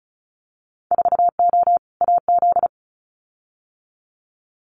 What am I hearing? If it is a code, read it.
Morse code, heard as a keyed tone, about 34 wpm, 700 Hz. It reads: 4O AZ
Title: Winter Field Day Exchanges Encore 35wpm